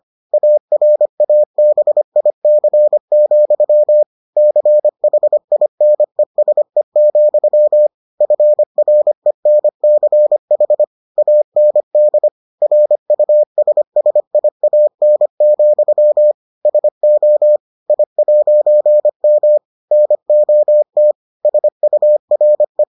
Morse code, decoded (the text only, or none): ARABIC, CHINESE, FRENCH AND RUSSIAN, SO I'M NOT SURE